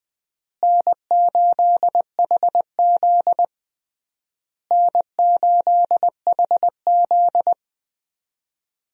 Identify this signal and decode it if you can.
Morse code, heard as a keyed tone, about 20 wpm, 700 Hz. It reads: N8HZ N8HZ
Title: USA Call Signs - 4 Characters - Repeated 3x - 20wpm